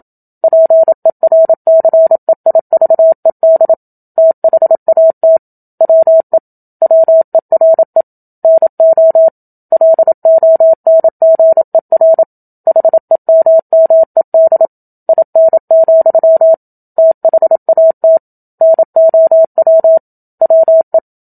Morse code, heard as a keyed tone, about 27 wpm, 650 Hz. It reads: PERCEIVED THAT WE WERE NO LONGER HEMMED IN, THAT NOW WE
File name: Book Two / 01-Chapter 1 - 27 WPM